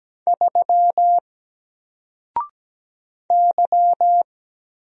Morse code, heard as a keyed tone, about 17 wpm, 700 Hz. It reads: 3 Y
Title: Single Letter-Number Alt 17wpm